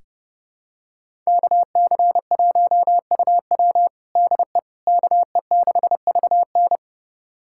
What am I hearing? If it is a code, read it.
Morse code, heard as a keyed tone, about 30 wpm, 700 Hz. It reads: KC1UW DE KE6VN
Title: USA Callsign DE Callsign - 5 Characters - 30wpm